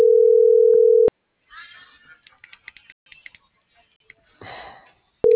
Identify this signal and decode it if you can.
Ringback tone (North America), 440 + 480 Hz.